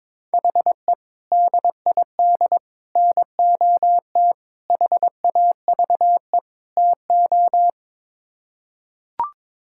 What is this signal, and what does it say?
Morse code, heard as a keyed tone, about 22 wpm, 700 Hz. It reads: HE DID NOT HAVE TO
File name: Build A Phrase - 5-Word Phrases - 22wpm